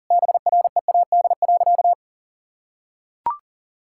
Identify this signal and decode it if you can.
Morse code, heard as a keyed tone, about 40 wpm, 700 Hz. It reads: BREAD.
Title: Sentences from Top 1600 Words 40wpm